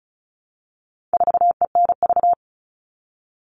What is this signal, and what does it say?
Morse code, heard as a keyed tone, about 34 wpm, 700 Hz. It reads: 4ENV
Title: All Cut Numbers - 4-Digit 35wpm